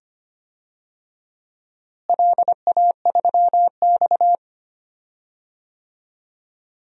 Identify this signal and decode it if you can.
Morse code, heard as a keyed tone, about 25 wpm, 700 Hz. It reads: LA3X
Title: DXCC-100 Call Signs - 4 Characters - Repeated 2x - 25wpm